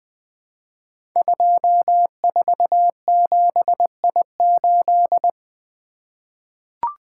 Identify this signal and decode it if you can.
Morse code, heard as a keyed tone, about 20 wpm, 700 Hz. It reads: 247I8
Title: Code Groups - 5 Letter:Numbers - 20wpm